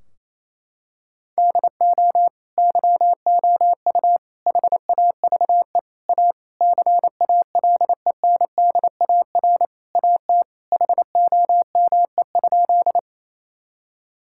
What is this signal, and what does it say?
Morse code, heard as a keyed tone, about 28 wpm, 700 Hz. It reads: DO YOU HAVE A CALENDAR AT HOME?
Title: Sentences from Top 1600 Words 28wpm